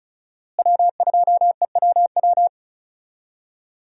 Morse code, heard as a keyed tone, about 36 wpm, 700 Hz. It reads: W2EWW
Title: USA Call Signs - 5 Characters - Repeated 2x - 35wpm